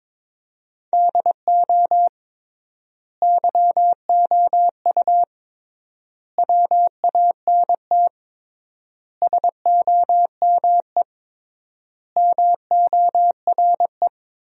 Morse code, heard as a keyed tone, about 22 wpm, 700 Hz. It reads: DO YOU WANT SOME MORE
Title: 3x Word-Spacing - Sentences from Top 300 Words - 22wpm